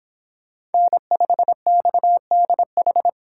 Morse code, heard as a keyed tone, about 31 wpm, 700 Hz. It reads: N5XDH